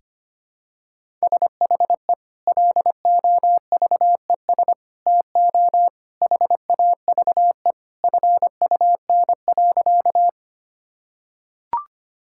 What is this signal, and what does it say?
Morse code, heard as a keyed tone, about 25 wpm, 700 Hz. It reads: SHE LOVES TO HAVE FUN.
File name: practice / 25wpm / Sentences with 2-Word Phrases - 25wpm